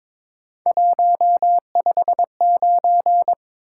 Morse code, heard as a keyed tone, about 22 wpm, 700 Hz. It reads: 159
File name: practice / 22wpm / Three Letter-Number Combinations - 22wpm